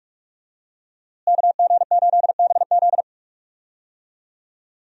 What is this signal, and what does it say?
Morse code, heard as a keyed tone, about 46 wpm, 700 Hz. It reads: KG8BZ
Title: USA Call Signs - 5 Characters - Repeated 2x - 45wpm